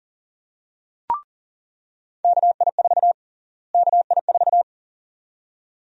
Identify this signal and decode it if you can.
Morse code, heard as a keyed tone, about 40 wpm, 700 Hz. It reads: KI4 KI4
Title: USA Call Sign Prefixes - Repeated 2x - 40wpm